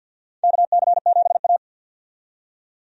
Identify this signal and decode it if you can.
Morse code, heard as a keyed tone, about 50 wpm, 700 Hz. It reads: KC7A
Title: USA Call Signs - 4 Characters - 50wpm